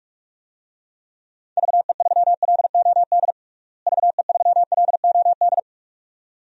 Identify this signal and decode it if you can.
Morse code, heard as a keyed tone, about 46 wpm, 700 Hz. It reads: VE3LOD VE3LOD
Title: DXCC-100 Call Signs - 6 Characters - Repeated 2x - 45wpm